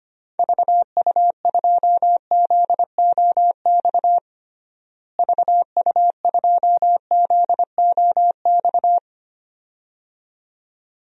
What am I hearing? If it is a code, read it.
Morse code, heard as a keyed tone, about 25 wpm, 700 Hz. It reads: VU2ZOX VU2ZOX
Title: DXCC-100 Call Signs - 6 Characters - Repeated 2x - 25wpm